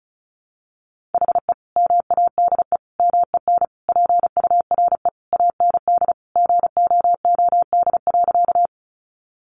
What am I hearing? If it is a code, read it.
Morse code, heard as a keyed tone, about 34 wpm, 700 Hz. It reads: HE MADE MEN PURE AND GOOD.
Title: Sentences from Top 1500 Words 35wpm